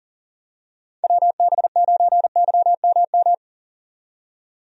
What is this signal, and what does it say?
Morse code, heard as a keyed tone, about 40 wpm, 700 Hz. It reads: WB9YMM
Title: USA Call Signs - 6 Characters - 40wpm